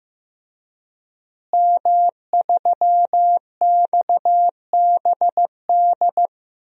Morse code, heard as a keyed tone, about 15 wpm, 700 Hz. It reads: M3XBD
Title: DXCC-100 Call Signs - 5 Characters - Repeated 3x - 15wpm